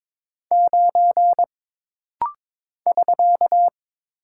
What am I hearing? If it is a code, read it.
Morse code, heard as a keyed tone, about 23 wpm, 700 Hz. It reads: 9 <SK>
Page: This rapid-fire timing is geared to push your ICR (Instant Character Recognition) skills.